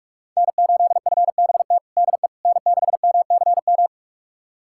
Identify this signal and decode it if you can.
Morse code, heard as a keyed tone, about 45 wpm, 700 Hz. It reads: N8FBT DE N6MCM
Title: USA Callsign DE Callsign - 5 Characters - 45wpm